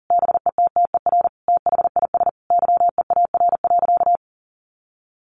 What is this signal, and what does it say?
Morse code, heard as a keyed tone, about 40 wpm, 700 Hz. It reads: BETTER THIS YEAR.
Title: Sentences from Top 1500 Words 40wpm